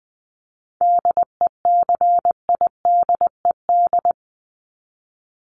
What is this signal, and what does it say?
Morse code, heard as a keyed tone, about 19 wpm, 700 Hz. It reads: DECIDED
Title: ED Suffix - Top 100 Words with Base Words 20wpm